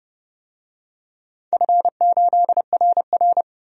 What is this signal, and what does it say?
Morse code, heard as a keyed tone, about 30 wpm, 700 Hz. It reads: F8RR
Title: DXCC-100 Call Signs - 4 Characters - 30wpm